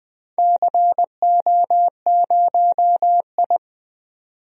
Morse code, heard as a keyed tone, about 21 wpm, 700 Hz. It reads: CO0I
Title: DXCC-100 Call Signs - 4 Characters - 20wpm